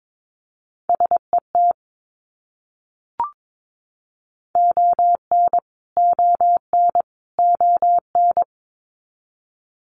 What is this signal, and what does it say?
Morse code, heard as a keyed tone, about 22 wpm, 700 Hz. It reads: SET ON ON ON
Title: Top 300 Words Repeated 22wpm